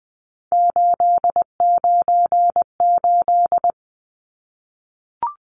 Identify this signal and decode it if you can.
Morse code, heard as a keyed tone, about 20 wpm, 700 Hz. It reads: 898
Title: Numbers - 3-Digit 20wpm